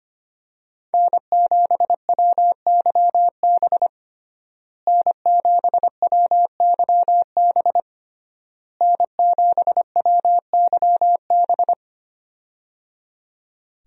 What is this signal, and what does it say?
Morse code, heard as a keyed tone, about 25 wpm, 700 Hz. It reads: N7WYB N7WYB N7WYB
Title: USA Call Signs - 5 Characters - Repeated 3x - 25wpm